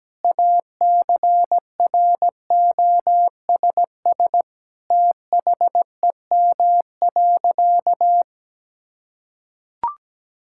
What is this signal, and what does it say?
Morse code, heard as a keyed tone, about 17 wpm, 700 Hz. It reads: ACROSS THEM.
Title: Sentences from Top 1000 Words - Review Part 5 - 17wpm